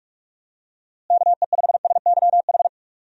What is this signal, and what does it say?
Morse code, heard as a keyed tone, about 41 wpm, 700 Hz. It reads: KE5SYH
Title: USA Call Signs - 6 Characters - Repeated 2x - 45wpm